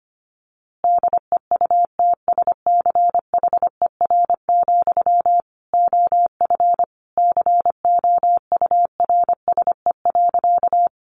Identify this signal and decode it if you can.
Morse code, heard as a keyed tone, about 25 wpm, 700 Hz. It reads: DEUTSCHER, OF COURSE.
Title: A Sound Of Thunder 25wpm